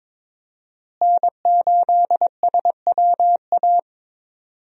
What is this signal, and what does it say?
Morse code, heard as a keyed tone, about 22 wpm, 700 Hz. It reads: N8SWA
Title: USA Call Signs - 5 Characters - Repeated 2x - 22wpm